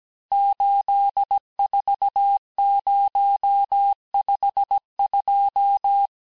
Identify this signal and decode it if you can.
Morse code, heard as a keyed tone, about 17 wpm, 800 Hz. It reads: 84052
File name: corrige-cours-1581-texte-chiffre-vitesse-16-mpm